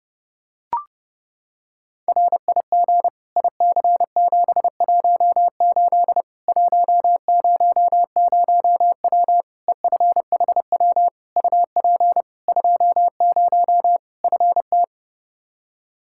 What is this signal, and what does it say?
Morse code, heard as a keyed tone, about 32 wpm, 700 Hz. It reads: RIG IC718 100W EFHW UP 20 FT
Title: Practice QSO - Exchange 2 - Info - 30wpm